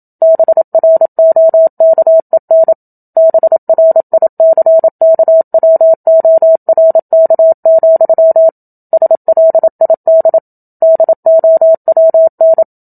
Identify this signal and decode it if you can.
Morse code, heard as a keyed tone, about 27 wpm, 650 Hz. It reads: BROKEN BRICKWORK, SLID DOWN
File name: Book Two / 01-Chapter 1 - 27 WPM